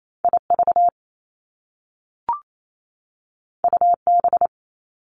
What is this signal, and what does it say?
Morse code, heard as a keyed tone, about 28 wpm, 700 Hz. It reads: IV UB
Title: Top Two Letter Combinations in Words 28wpm